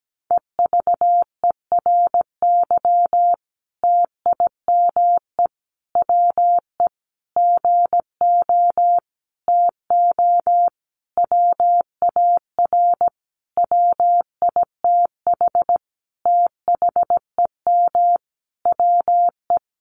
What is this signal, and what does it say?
Morse code, heard as a keyed tone, about 17 wpm, 700 Hz. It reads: EVERY TIME WE GO TO WAR WITH THEM WE
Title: Sentences from Top 1300 Words 17wpm